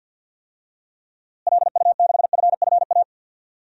Morse code, heard as a keyed tone, about 52 wpm, 700 Hz. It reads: LU6FFA
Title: DXCC-100 Call Signs - 6 Characters - Repeated 3x - 50wpm